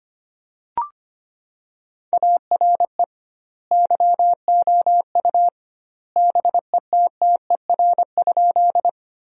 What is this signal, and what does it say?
Morse code, heard as a keyed tone, about 25 wpm, 700 Hz. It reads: ARE YOU BETTER?
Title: 2x Word-Spacing - Sentences from Top 400 Words - 25wpm